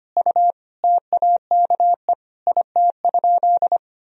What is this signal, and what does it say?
Morse code, heard as a keyed tone, about 25 wpm, 700 Hz. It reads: U TAKE IT?
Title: Sentences from Top 200 Words and Abbreviations 25wpm